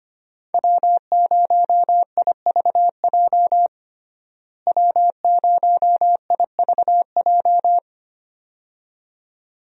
Morse code, heard as a keyed tone, about 25 wpm, 700 Hz. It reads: W0IVJ W0IVJ
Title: USA Call Signs - 5 Characters - Repeated 2x - 25wpm